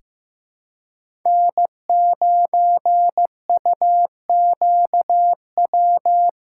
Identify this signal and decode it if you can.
Morse code, heard as a keyed tone, about 16 wpm, 700 Hz. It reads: N9UQW
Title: USA Callsign DE Callsign - 5 Characters - 15wpm